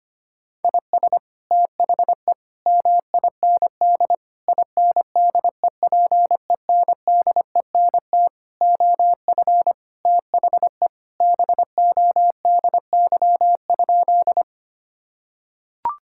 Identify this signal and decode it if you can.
Morse code, heard as a keyed tone, about 25 wpm, 700 Hz. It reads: IS THE MIND INDEPENDENT OF THE BODY?
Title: Sentences from Top 1000 Words - Review Part 6 - 25wpm